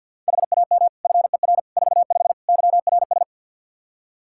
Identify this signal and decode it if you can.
Morse code, heard as a keyed tone, about 52 wpm, 700 Hz. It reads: HAM FER 45 YRS
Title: Practice QSO - Exchange 3 - More Info - 50wpm